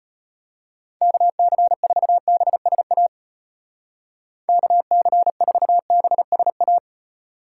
Morse code, heard as a keyed tone, about 32 wpm, 700 Hz. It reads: KC4BSA KC4BSA